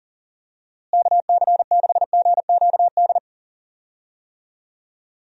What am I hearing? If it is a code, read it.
Morse code, heard as a keyed tone, about 40 wpm, 700 Hz. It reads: KC6GQD
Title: USA Call Signs - 6 Characters - Repeated 2x - 40wpm